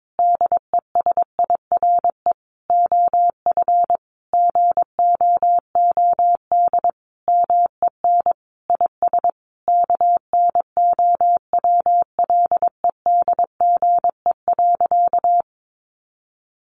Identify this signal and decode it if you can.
Morse code, heard as a keyed tone, about 22 wpm, 700 Hz. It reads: DESIRE OF GOOD MEN IS KNOWLEDGE.
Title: Sentences from Top 1100 Words 22wpm